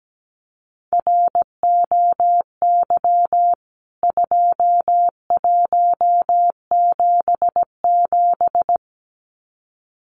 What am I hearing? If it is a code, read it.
Morse code, heard as a keyed tone, about 17 wpm, 700 Hz. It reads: ROY 2177
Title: CWT Exchange 17wpm